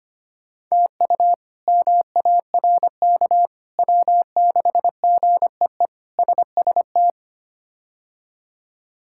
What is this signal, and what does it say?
Morse code, heard as a keyed tone, about 25 wpm, 700 Hz. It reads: TU MARK W6GEE SST
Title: K1USN SST QSO Practice - 25wpm